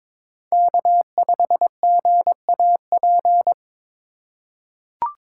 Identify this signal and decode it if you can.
Morse code, heard as a keyed tone, about 22 wpm, 700 Hz. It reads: K5GAP
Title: USA Call Signs - 5 Characters - 2x Character Spacing - 22wpm